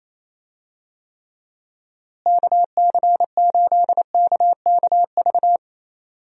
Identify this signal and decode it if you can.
Morse code, heard as a keyed tone, about 28 wpm, 700 Hz. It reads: KC8KKV
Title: USA Call Signs - 6 Characters - Repeated 2x - 28wpm